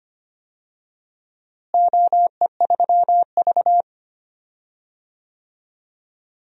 Morse code, heard as a keyed tone, about 25 wpm, 700 Hz. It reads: OE3V
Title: DXCC-100 Call Signs - 4 Characters - Repeated 3x - 25wpm